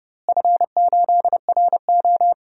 Morse code, heard as a keyed tone, about 29 wpm, 700 Hz. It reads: F8RO
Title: DXCC-100 Call Signs - 4 Characters - Repeated 3x - 30wpm